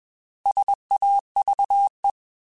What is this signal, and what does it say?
Morse code, heard as a keyed tone, about 21 wpm, 800 Hz. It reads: SAVE
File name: Alphabetical / Alphabetical mp3s / Ss @20